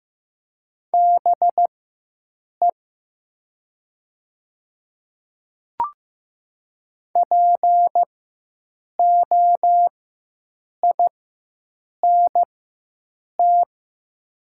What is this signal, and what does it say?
Morse code, heard as a keyed tone, about 15 wpm, spaced as for 7 wpm, 700 Hz. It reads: BE POINT
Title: 4x Character Spacing - Top 200 Words - 15wpm